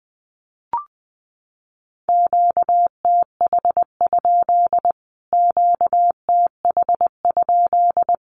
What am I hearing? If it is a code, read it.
Morse code, heard as a keyed tone, about 20 wpm, 700 Hz. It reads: QTH? QTH?
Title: Q-Codes Repeated 3x 20wpm